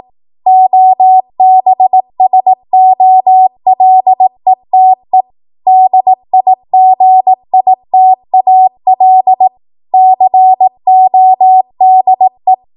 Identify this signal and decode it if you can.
Morse code, heard as a keyed tone, about 18 wpm, 750 Hz. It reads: OBSOLETE DIGITAL CODE
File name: Morse / 18WPM